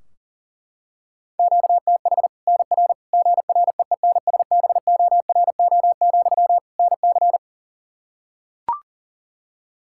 Morse code, heard as a keyed tone, about 40 wpm, 700 Hz. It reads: QTH NR GREENSBORO, NC
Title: Practice QSO - Exchange 1 - Beginning - 40wpm